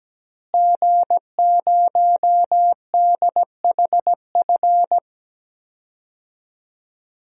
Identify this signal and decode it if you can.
Morse code, heard as a keyed tone, about 17 wpm, 700 Hz. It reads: G0DHF
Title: DXCC-100 Call Signs - 5 Characters - Repeated 2x - 17wpm